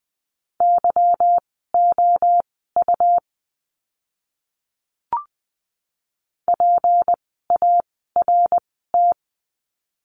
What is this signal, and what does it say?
Morse code, heard as a keyed tone, about 20 wpm, spaced as for 15 wpm, 700 Hz. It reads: YOU PART
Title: 153 Lesson 25f - Letter p - Words 20wpm